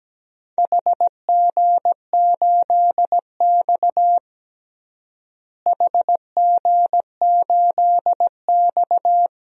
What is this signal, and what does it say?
Morse code, heard as a keyed tone, about 17 wpm, 700 Hz. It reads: HG8X HG8X
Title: DXCC-100 Call Signs - 4 Characters - Repeated 3x - 17wpm